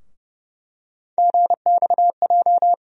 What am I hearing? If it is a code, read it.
Morse code, heard as a keyed tone, about 31 wpm, 700 Hz. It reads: GXJ
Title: USA Call Sign Postfixes - 3 Letters 30wpm